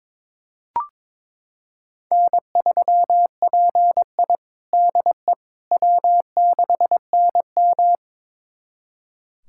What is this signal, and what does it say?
Morse code, heard as a keyed tone, about 22 wpm, 700 Hz. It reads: N3PI DE W6NM
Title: USA Callsign DE Callsign - 4 Characters - 22wpm